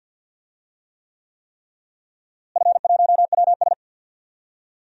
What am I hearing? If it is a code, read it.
Morse code, heard as a keyed tone, about 49 wpm, 700 Hz. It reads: F1WS
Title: DXCC-100 Call Signs - 4 Characters - Repeated 3x - 50wpm